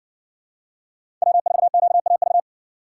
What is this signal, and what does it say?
Morse code, heard as a keyed tone, about 60 wpm, 700 Hz. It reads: W4QAV